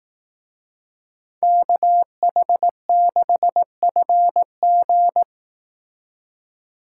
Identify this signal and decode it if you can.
Morse code, heard as a keyed tone, about 18 wpm, 700 Hz. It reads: KH6FG
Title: Speed Racing - USA 2x2 Call Signs 15wpm